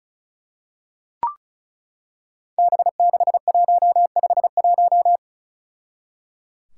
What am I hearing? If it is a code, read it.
Morse code, heard as a keyed tone, about 34 wpm, 700 Hz. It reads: B6151
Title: Five Challenging Letter-Number Combinations 35wpm